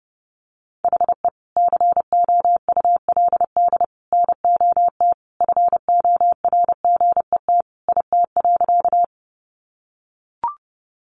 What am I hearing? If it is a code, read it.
Morse code, heard as a keyed tone, about 30 wpm, 700 Hz. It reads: HE COULD NOT FORGET IT.
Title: Sentences from Top 1200 Words 30wpm